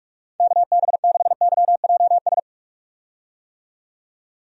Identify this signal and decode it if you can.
Morse code, heard as a keyed tone, about 42 wpm, 700 Hz. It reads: KB6YJS
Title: USA Call Signs - 6 Characters - 45wpm